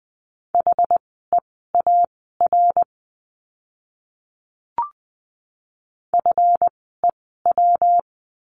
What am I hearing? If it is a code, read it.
Morse code, heard as a keyed tone, about 20 wpm, spaced as for 15 wpm, 700 Hz. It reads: HEAR FEW
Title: 206 Lesson 33f - Letter j - Words 20wpm